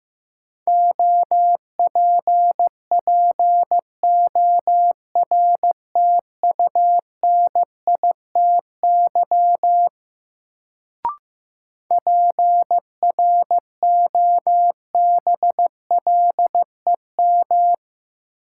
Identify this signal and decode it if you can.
Morse code, heard as a keyed tone, about 15 wpm, 700 Hz. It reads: OPPORTUNITY PROBLEM
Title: Sentences from Top 1000 Words - New Word Drill - 15wpm